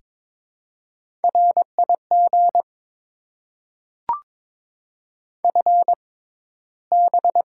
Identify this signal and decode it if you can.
Morse code, heard as a keyed tone, about 22 wpm, spaced as for 14 wpm, 700 Hz. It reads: RIG FB
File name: Top 90 QSO Elements - 6x Character Spacing - 22wpm